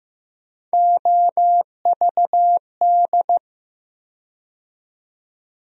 Morse code, heard as a keyed tone, about 15 wpm, 700 Hz. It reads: OVD
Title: USA Call Sign Postfixes - 3 Letters 15wpm